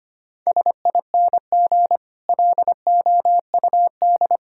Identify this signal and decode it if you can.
Morse code, heard as a keyed tone, about 25 wpm, 690 Hz. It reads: SING LOUD
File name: Easy Sentences with Hints - 25wpm